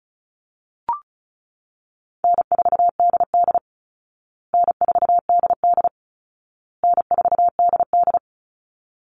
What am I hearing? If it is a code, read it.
Morse code, heard as a keyed tone, about 34 wpm, 700 Hz. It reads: N4DD N4DD N4DD
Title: Call Signs Repeated 2x - 35wpm